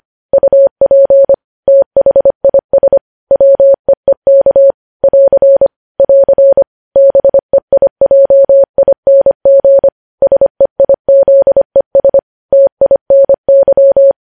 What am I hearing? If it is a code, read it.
Morse code, heard as a keyed tone, about 25 wpm, 550 Hz. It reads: UP THIS WEEK <AR> <AR> BEIJING SEIZES TINY
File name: podcast / News headlines-1x-25x25